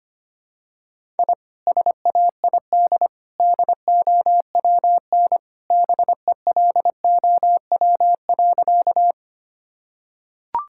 Morse code, heard as a keyed tone, about 25 wpm, 700 Hz. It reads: I SAID DOWN BELOW.
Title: Sentences from Top 1000 Words - Review Part 5 - 25wpm